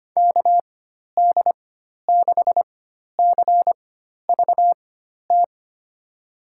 Morse code, heard as a keyed tone, about 25 wpm, spaced as for 12 wpm, 700 Hz. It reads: KD6CVT
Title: USA Call Signs - 6 Characters - 4x Character Spacing - 25wpm